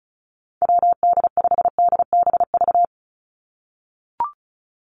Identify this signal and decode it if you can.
Morse code, heard as a keyed tone, about 34 wpm, 700 Hz. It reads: WD5DBV
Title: Call Signs - 35wpm